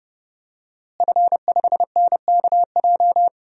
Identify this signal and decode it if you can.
Morse code, heard as a keyed tone, about 31 wpm, 700 Hz. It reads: F5NKJ
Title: DXCC-100 Call Signs - 5 Characters - Repeated 3x - 30wpm